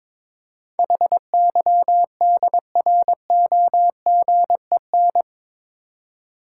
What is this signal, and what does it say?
Morse code, heard as a keyed tone, about 21 wpm, 700 Hz. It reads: HYDROGEN
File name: practice / 22wpm / Eight Letter Words - 22wpm